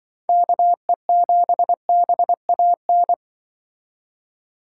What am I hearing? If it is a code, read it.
Morse code, heard as a keyed tone, about 24 wpm, 700 Hz. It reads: KE7BAN
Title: Speed Racing - USA 2x3 Call Signs 20wpm